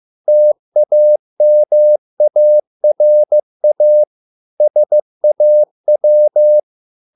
Morse code, heard as a keyed tone, about 15 wpm, 600 Hz. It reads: TAMARA SAW